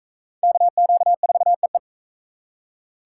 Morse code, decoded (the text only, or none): KQ4EE